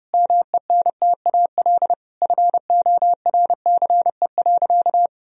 Morse code, heard as a keyed tone, about 32 wpm, 700 Hz. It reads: MENTAL FORCE.
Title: Sentences from Top 1000 Words - Review Part 4 - 30wpm